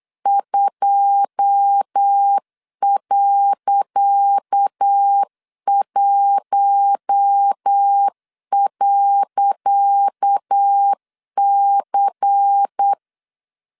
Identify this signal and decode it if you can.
Morse code, heard as a keyed tone, about 8 wpm, 800 Hz. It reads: 2.1.C
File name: LEZIONI / 10 SIGNS / 10C DOT 8wpm 3649